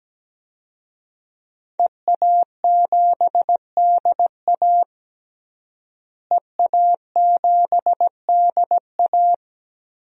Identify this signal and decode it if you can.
Morse code, heard as a keyed tone, about 17 wpm, 700 Hz. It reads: EA7DA EA7DA
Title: DXCC-100 Call Signs - 5 Characters - Repeated 3x - 17wpm